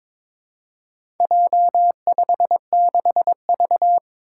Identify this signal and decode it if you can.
Morse code, heard as a keyed tone, about 22 wpm, 700 Hz. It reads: J56V
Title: Four Challenging Letter-Number Combinations 22wpm